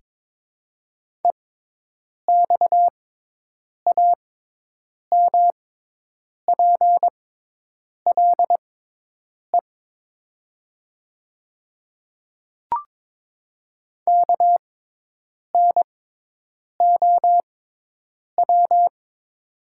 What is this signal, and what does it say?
Morse code, heard as a keyed tone, about 22 wpm, spaced as for 8 wpm, 700 Hz. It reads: EXAMPLE KNOW
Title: 6x Character Spacing - Top 300 Words - 22wpm